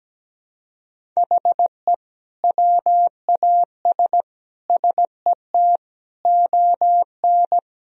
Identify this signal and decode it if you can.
Morse code, heard as a keyed tone, about 17 wpm, 700 Hz. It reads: HE WAS SET ON
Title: Sentences from Top 1000 Words - Review Part 1 - 17wpm